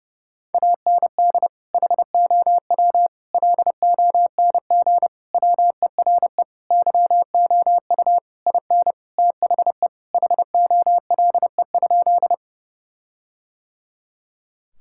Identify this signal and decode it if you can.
Morse code, heard as a keyed tone, about 30 wpm, 700 Hz. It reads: AND HOW LONG WERE YOU IN THE HOLE?
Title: Sentences from Top 1600 Words 30wpm